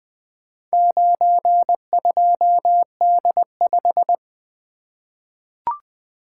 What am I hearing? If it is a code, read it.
Morse code, heard as a keyed tone, about 20 wpm, 700 Hz. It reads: 92D5
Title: Four Letter-Number Combinations - 20wpm